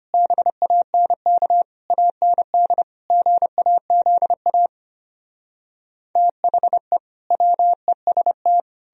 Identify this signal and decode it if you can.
Morse code, heard as a keyed tone, about 33 wpm, 700 Hz. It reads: BANK AND GAZA THE WEST